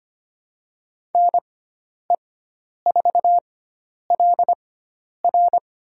Morse code, heard as a keyed tone, about 25 wpm, spaced as for 10 wpm, 700 Hz. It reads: NE4LR
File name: Lesson 16g - Letter c - Call Signs 25wpm